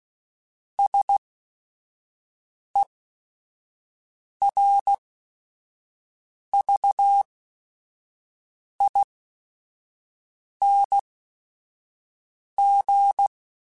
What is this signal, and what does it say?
Morse code, heard as a keyed tone, about 16 wpm, spaced as for 5 wpm, 800 Hz. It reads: SERVING